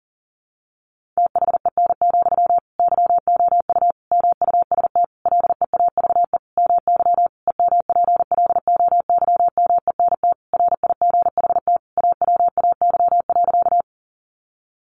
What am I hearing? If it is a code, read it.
Morse code, heard as a keyed tone, about 40 wpm, 700 Hz. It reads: THEN, YOU MUST LEAVE MY EMPLOYMENT RIGHT AWAY.
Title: Sentences from Top 1500 Words 40wpm